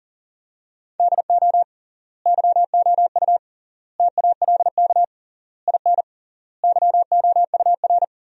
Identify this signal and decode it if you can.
Morse code, heard as a keyed tone, about 40 wpm, 700 Hz. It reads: DO YOU TALK IN YOUR
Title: 3x Word-Spacing - Sentences from Top 500 Words - 40wpm